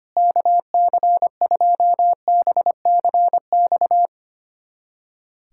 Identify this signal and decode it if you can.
Morse code, heard as a keyed tone, about 25 wpm, 700 Hz. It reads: KC2BCX
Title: USA Callsign DE Callsign - 6 Characters - 25wpm